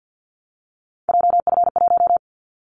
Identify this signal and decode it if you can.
Morse code, heard as a keyed tone, about 50 wpm, 700 Hz. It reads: JF1
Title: Callsign Prefixes for Top 25 DXCC - 50wpm